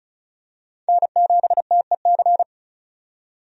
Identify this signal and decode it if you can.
Morse code, heard as a keyed tone, about 34 wpm, 700 Hz. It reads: N7TEC
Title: USA Call Signs - 5 Characters - 35wpm